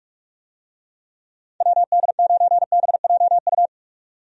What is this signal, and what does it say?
Morse code, heard as a keyed tone, about 46 wpm, 700 Hz. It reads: WD9BJU
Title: USA Call Signs - 6 Characters - 45wpm